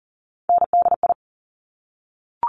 Morse code, heard as a keyed tone, about 41 wpm, 700 Hz. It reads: NDI
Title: Top Three Letter Combinations in Words 40wpm